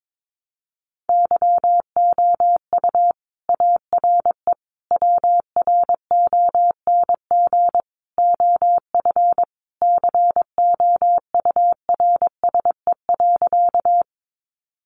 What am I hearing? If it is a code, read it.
Morse code, heard as a keyed tone, about 22 wpm, 700 Hz. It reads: YOU ARE WRONG OF COURSE.
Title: Sentences from Top 900 Words 22wpm